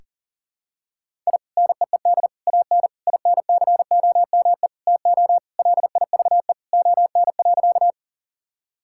Morse code, heard as a keyed tone, about 37 wpm, 700 Hz. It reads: I NEED AN INCOME TO LIVE ON.
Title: Sentences from Top 1000 Words - Review Part 5 - 40wpm